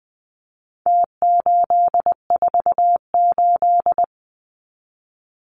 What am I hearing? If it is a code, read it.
Morse code, heard as a keyed tone, about 20 wpm, 700 Hz. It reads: T848
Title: Numbers - Contest Style - 4-Digit 20wpm